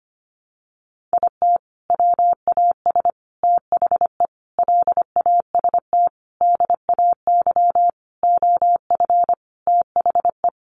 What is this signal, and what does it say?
Morse code, heard as a keyed tone, about 25 wpm, 700 Hz. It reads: IT WAS THE LAST DAY OF THE
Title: Sentences from Top 300 Words 25wpm